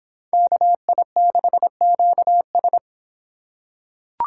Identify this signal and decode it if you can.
Morse code, heard as a keyed tone, about 26 wpm, 700 Hz. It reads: KI6QS